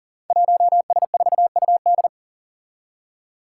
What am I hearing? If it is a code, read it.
Morse code, heard as a keyed tone, about 40 wpm, 700 Hz. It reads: 1S4UD
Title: Five Challenging Letter-Number Combinations 40wpm